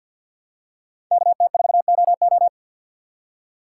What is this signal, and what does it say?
Morse code, heard as a keyed tone, about 53 wpm, 700 Hz. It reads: KT4OO
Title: USA Call Signs - 5 Characters - 2x Character Spacing - 50wpm